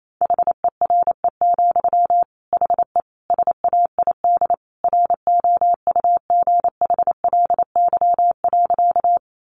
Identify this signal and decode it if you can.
Morse code, heard as a keyed tone, about 28 wpm, 700 Hz. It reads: HERE, HE SAID ROUGHLY.
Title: Sentences from Top 1500 Words 28wpm